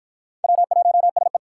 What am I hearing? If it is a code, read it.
Morse code, heard as a keyed tone, about 47 wpm, 700 Hz. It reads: W1SE